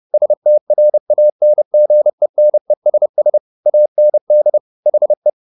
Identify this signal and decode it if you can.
Morse code, heard as a keyed tone, about 30 wpm, 600 Hz. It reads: STRANGENESS AND HE